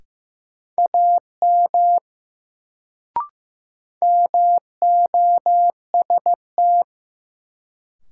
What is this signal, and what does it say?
Morse code, heard as a keyed tone, about 15 wpm, 700 Hz. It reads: AM MOST
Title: Sets of 2 Words - New Word Drill - 15wpm